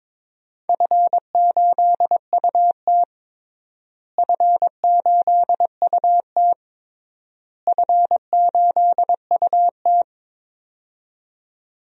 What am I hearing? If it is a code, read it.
Morse code, heard as a keyed tone, about 22 wpm, 700 Hz. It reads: F8UT F8UT F8UT
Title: DXCC-100 Call Signs - 4 Characters - Repeated 3x - 22wpm